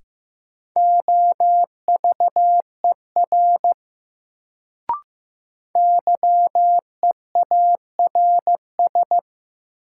Morse code, heard as a keyed tone, about 15 wpm, 700 Hz. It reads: OVER YEARS
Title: Sets of 2 Words - New Word Drill - 15wpm